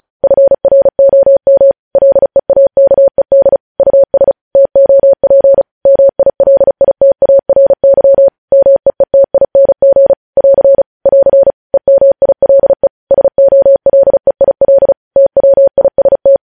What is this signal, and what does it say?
Morse code, heard as a keyed tone, about 36 wpm, 550 Hz. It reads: FROM LEAKED US TOP MILITARY MEETING <AR> <AR> EMILE SOLEIL TWIST
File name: podcast / News headlines-1x-35x35